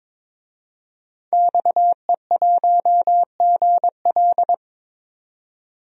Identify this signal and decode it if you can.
Morse code, heard as a keyed tone, about 21 wpm, 700 Hz. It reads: XE1GL
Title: DXCC-100 Call Signs - 5 Characters - 22wpm